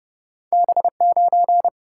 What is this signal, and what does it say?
Morse code, heard as a keyed tone, about 29 wpm, 700 Hz. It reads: B9
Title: Single-Letter Single-Number 30wpm